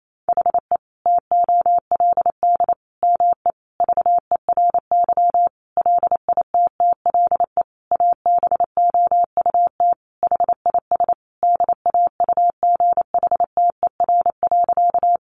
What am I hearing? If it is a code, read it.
Morse code, heard as a keyed tone, about 28 wpm, 700 Hz. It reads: HE TOLD ME VERY LITTLE ABOUT HIS DAUGHTER.
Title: Sentences from Top 1500 Words 28wpm